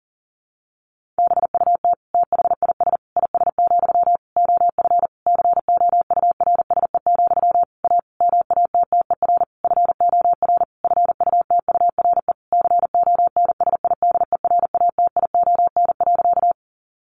Morse code, heard as a keyed tone, about 37 wpm, 700 Hz. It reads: BUT THIS IS, OF COURSE, A MATTER FOR FUTURE CONSIDERATION.
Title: Sentences from Top 1500 Words 40wpm